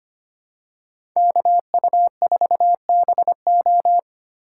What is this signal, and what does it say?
Morse code, heard as a keyed tone, about 25 wpm, 700 Hz. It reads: KU4BO
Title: USA Call Signs - 5 Characters - 3x Character Spacing - 25wpm